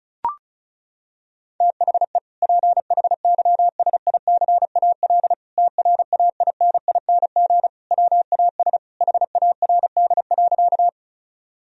Morse code, heard as a keyed tone, about 36 wpm, 700 Hz. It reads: THE PHYSICAL TRAINING WAS HARD.
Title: Sentences from Top 1000 Words - Review Part 4 - 35wpm